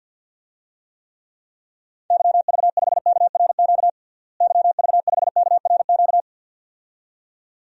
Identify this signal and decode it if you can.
Morse code, heard as a keyed tone, about 52 wpm, 700 Hz. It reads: YV5KRQ YV5KRQ
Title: DXCC-100 Call Signs - 6 Characters - Repeated 2x - 50wpm